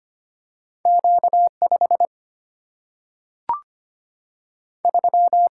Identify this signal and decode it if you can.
Morse code, heard as a keyed tone, about 25 wpm, 700 Hz. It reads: Q5 3
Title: Single-Letter Single-Number 25wpm